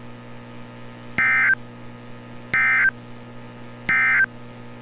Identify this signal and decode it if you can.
Emergency Alert System (SAME) tones, heard as harsh buzzing bursts. It is heard from 1.1 to 4.3 s.
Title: NOAA Weather Radio - Altoona, PA